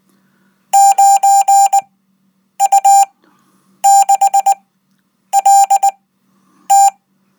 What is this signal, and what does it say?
Morse code, heard as a keyed tone, about 19 wpm, spaced as for 8 wpm, 790 Hz. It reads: 9U6LT
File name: Code